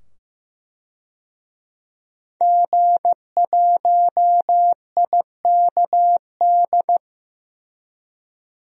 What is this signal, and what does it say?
Morse code, heard as a keyed tone, about 15 wpm, 700 Hz. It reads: G1IKD
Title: DXCC-100 Call Signs - 5 Characters - 15wpm